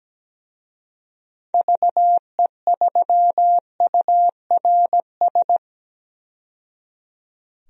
Morse code, heard as a keyed tone, about 17 wpm, 700 Hz. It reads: VE3URS
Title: DXCC-100 Call Signs - 6 Characters - 17wpm